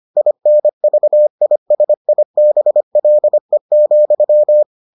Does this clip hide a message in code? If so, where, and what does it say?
Yes: Morse code from 0.2 to 4.6 s, heard as a keyed tone, about 25 wpm, 600 Hz. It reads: INVISIBLE,